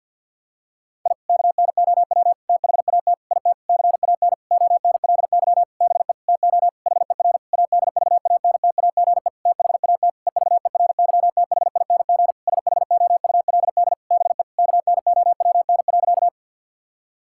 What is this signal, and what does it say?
Morse code, heard as a keyed tone, about 50 wpm, 700 Hz. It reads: I KNOW THAT IT CAN ONLY BE TO HER ADVANTAGE THAT EVERYTHING SHOULD BE KNOWN.
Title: Sentences from Top 1000 Words - Review Part 6 - 50wpm